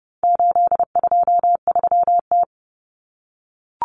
Morse code, heard as a keyed tone, about 30 wpm, 700 Hz. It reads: 823T
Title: Common Cut Numbers - 4-Digit 30wpm